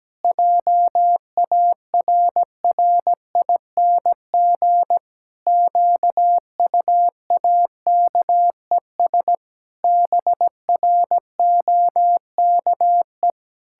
Morse code, heard as a keyed tone, about 17 wpm, 700 Hz. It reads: JARRING QUAKES BROKE
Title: Pangram Challenge v2 - 17wpm